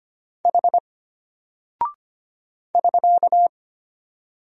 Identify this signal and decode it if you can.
Morse code, heard as a keyed tone, about 25 wpm, 700 Hz. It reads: H <SK>
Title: Single Character (Letter:Number:Punctuation:Prosign) Alt 25wpm